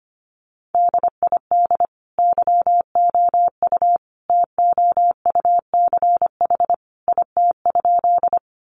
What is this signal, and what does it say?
Morse code, heard as a keyed tone, about 25 wpm, 700 Hz. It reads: DID YOU TOUCH IT?
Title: Sentences from Top 1200 Words 25wpm